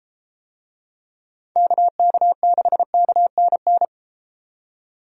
Morse code, heard as a keyed tone, about 32 wpm, 700 Hz. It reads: KK6KNN